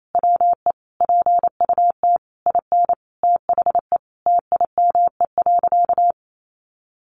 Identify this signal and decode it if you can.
Morse code, heard as a keyed tone, about 28 wpm, 700 Hz. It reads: WE PUT IN THE TIME.
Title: Sentences from Top 200 Words 28wpm